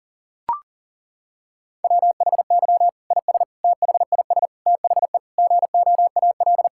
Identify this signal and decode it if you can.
Morse code, heard as a keyed tone, about 40 wpm, 700 Hz. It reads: WHY IS THIS THE GOAL
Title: Sentences from Top 1000 Words - Review Part 5 - 40wpm